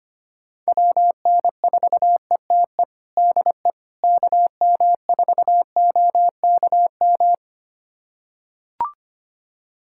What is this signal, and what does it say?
Morse code, heard as a keyed tone, about 25 wpm, 700 Hz. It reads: WN4ETE DE KM4OKM
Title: USA Callsign DE Callsign 25wpm